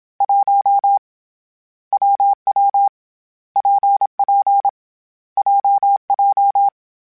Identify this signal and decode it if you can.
Morse code, heard as a keyed tone, about 27 wpm, 800 Hz. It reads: 1 WW PP JJ
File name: Difficult Combos / Difficult Combos-mp3s / WPJ WPJ1 @25